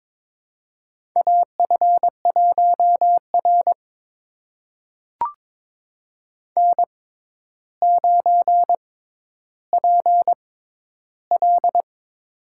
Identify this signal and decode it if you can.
Morse code, heard as a keyed tone, about 22 wpm, spaced as for 11 wpm, 700 Hz. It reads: AF1R N9PL
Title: USA Call Signs - 4 Characters - 6x Character Spacing - 22wpm